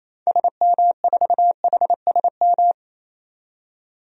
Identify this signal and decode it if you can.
Morse code, heard as a keyed tone, about 28 wpm, 700 Hz. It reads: SM4HSM
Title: DXCC-100 Call Signs - 6 Characters - 28wpm